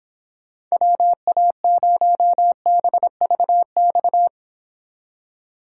Morse code, heard as a keyed tone, about 26 wpm, 700 Hz. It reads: WA0BVX